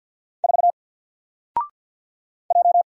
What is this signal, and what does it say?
Morse code, heard as a keyed tone, about 50 wpm, 700 Hz. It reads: 4 J
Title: Single Letter-Number Alt 50wpm